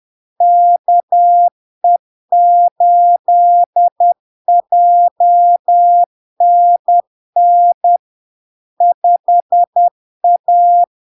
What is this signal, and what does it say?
Morse code, heard as a keyed tone, about 10 wpm, 700 Hz. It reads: KE8JNN 5A
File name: LARK-FD-2022-Log-in-CW-10wpm-700-Hz